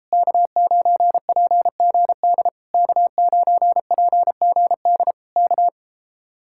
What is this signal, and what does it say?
Morse code, heard as a keyed tone, about 33 wpm, 700 Hz. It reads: K9PGD K9PGD K